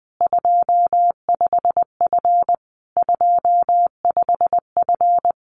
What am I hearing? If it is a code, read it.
Morse code, heard as a keyed tone, about 20 wpm, 700 Hz. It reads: 25F 25F
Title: Common QSO Phrases Encore 20wpm